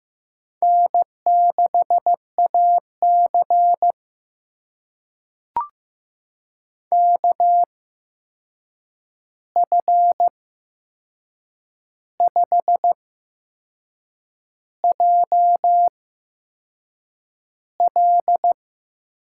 Morse code, heard as a keyed tone, about 15 wpm, spaced as for 6 wpm, 700 Hz. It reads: N6AC KF5JL
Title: USA Call Signs - 8x Character Spacing - 15wpm